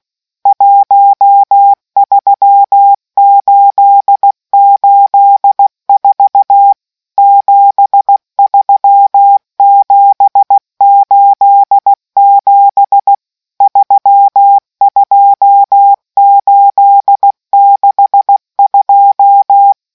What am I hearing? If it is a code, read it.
Morse code, heard as a keyed tone, about 16 wpm, 800 Hz. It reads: 13884 73787 32862